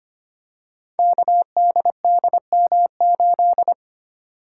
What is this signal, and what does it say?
Morse code, heard as a keyed tone, about 25 wpm, 700 Hz. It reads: KDDM8
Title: Code Groups - 5 Letter:Numbers - 25wpm